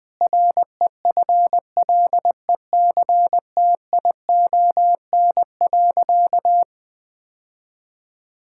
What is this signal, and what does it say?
Morse code, heard as a keyed tone, about 20 wpm, 700 Hz. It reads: REFLECTION.
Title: Difficult Sentences with Hints - 20wpm